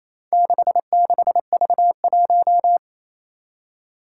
Morse code, heard as a keyed tone, about 27 wpm, 700 Hz. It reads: 66V1
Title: Four Challenging Letter-Number Combinations 28wpm